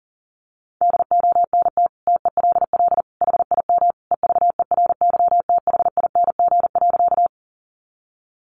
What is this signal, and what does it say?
Morse code, heard as a keyed tone, about 40 wpm, 700 Hz. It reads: DONT TELL HIM EVERYTHING.
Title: Sentences from Top 700 Words 40wpm